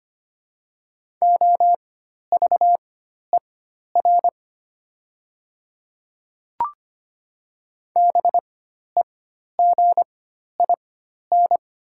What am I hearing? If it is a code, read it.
Morse code, heard as a keyed tone, about 25 wpm, spaced as for 12 wpm, 700 Hz. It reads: OVER BEGIN
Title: 4x Character Spacing - Top 300 Words - 25wpm